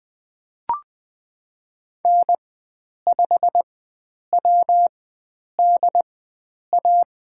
Morse code, heard as a keyed tone, about 20 wpm, spaced as for 9 wpm, 700 Hz. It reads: N5WDA
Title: USA Call Signs - 5 Characters - 4x Character Spacing - 20wpm